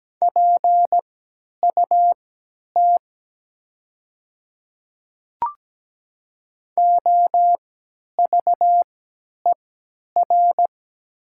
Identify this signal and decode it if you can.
Morse code, heard as a keyed tone, about 17 wpm, spaced as for 10 wpm, 700 Hz. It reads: PUT OVER
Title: 3x Character Spacing - Top 200 Words - 17wpm